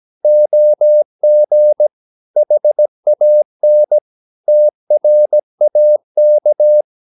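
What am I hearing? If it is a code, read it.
Morse code, heard as a keyed tone, about 17 wpm, 600 Hz. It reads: OG HAN TRAK
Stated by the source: Slow (QRS) Morse practice at 17 wpm, from CW med Gnister.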